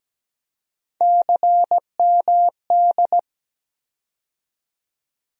Morse code, heard as a keyed tone, about 17 wpm, 700 Hz. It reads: CMD
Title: USA Call Sign Postfixes - 3 Letters 17wpm